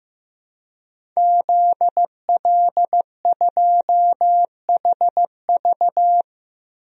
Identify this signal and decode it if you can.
Morse code, heard as a keyed tone, about 15 wpm, 700 Hz. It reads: ZL2HV
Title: DXCC-100 Call Signs - 5 Characters - Repeated 3x - 15wpm